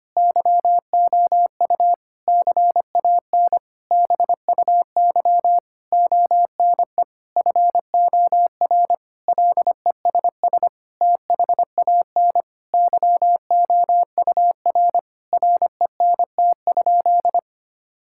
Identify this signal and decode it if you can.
Morse code, heard as a keyed tone, about 25 wpm, 700 Hz. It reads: YOU CAN BUY ONE FOR LESS THAN YOUR RENT?
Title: Sentences from Top 1000 Words - Review Part 5 - 25wpm